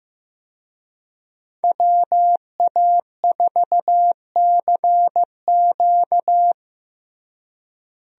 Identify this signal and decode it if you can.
Morse code, heard as a keyed tone, about 15 wpm, 700 Hz. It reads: WA4CQ
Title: USA Call Signs - 5 Characters - Repeated 2x - 15wpm